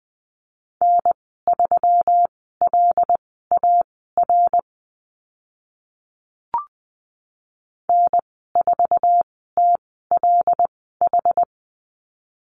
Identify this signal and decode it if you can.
Morse code, heard as a keyed tone, about 20 wpm, spaced as for 14 wpm, 700 Hz. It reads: N3LAR N4TLH
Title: 140 Lesson 23g - Letter f - Call Signs 20wpm